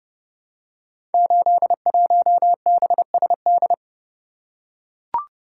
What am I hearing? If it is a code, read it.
Morse code, heard as a keyed tone, about 28 wpm, 700 Hz. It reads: 81BSD
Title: Code Groups - 5 Letter:Numbers - 30wpm